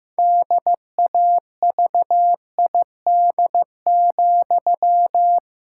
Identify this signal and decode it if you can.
Morse code, heard as a keyed tone, about 15 wpm, 700 Hz. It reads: DAVID,
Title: Pangram Challenge v2 - 15wpm